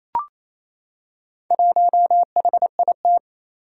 Morse code, heard as a keyed tone, about 29 wpm, 700 Hz. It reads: 1HIT
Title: Four Challenging Letter-Number Combinations 28wpm